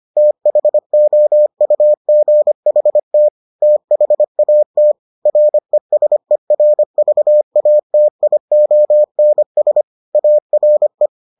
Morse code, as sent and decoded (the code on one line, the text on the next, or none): - .... --- ..- --. .... - / - .... .- - / .-. ... .-. ...- .- - .. --- -. ... / .- .-. .
THOUGHT THAT RESERVATIONS ARE